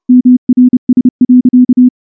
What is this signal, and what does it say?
Morse code, heard as a keyed tone, about 29 wpm, 250 Hz. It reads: MRS.
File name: morse / morse-training-30WPM